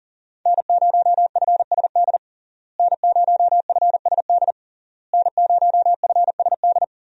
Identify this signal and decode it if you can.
Morse code, heard as a keyed tone, about 40 wpm, 700 Hz. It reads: N0FSD N0FSD N0FSD
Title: USA Call Signs - 5 Characters - Repeated 3x - 40wpm